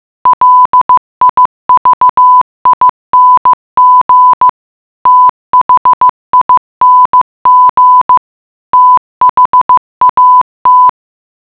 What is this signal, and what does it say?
Morse code, heard as a keyed tone, about 15 wpm, 1000 Hz. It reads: LIVING THING THAT
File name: mutation.15wpm